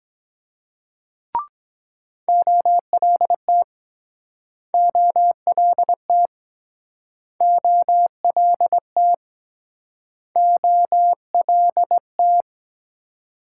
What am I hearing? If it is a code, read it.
Morse code, heard as a keyed tone, about 24 wpm, 700 Hz. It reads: OLT OLT OLT OLT